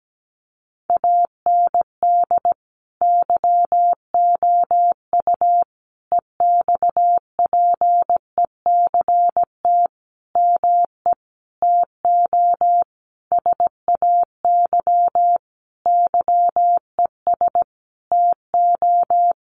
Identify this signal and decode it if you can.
Morse code, heard as a keyed tone, about 17 wpm, 700 Hz. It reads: AND YOU EXPECT ME TO SAY YES TO
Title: Sentences from Top 1100 Words 17wpm